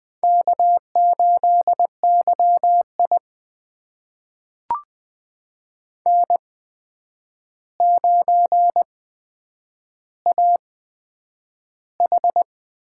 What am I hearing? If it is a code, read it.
Morse code, heard as a keyed tone, about 20 wpm, spaced as for 9 wpm, 700 Hz. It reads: K8YI N9AH
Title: USA Call Signs - 4 Characters - 8x Character Spacing - 20wpm